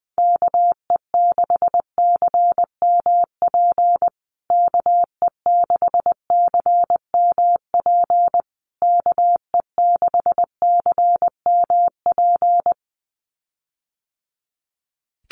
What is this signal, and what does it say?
Morse code, heard as a keyed tone, about 20 wpm, 700 Hz. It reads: KE6CMP KE6CMP KE6CMP
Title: USA Call Signs Repeated 3x 20wpm